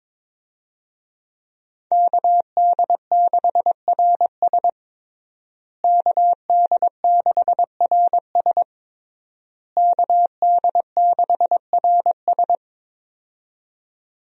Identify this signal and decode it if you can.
Morse code, heard as a keyed tone, about 22 wpm, 700 Hz. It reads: KD6RS KD6RS KD6RS
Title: USA Call Signs - 5 Characters - Repeated 3x - 22wpm